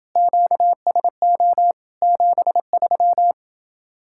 Morse code, heard as a keyed tone, about 29 wpm, 700 Hz. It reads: QSO 73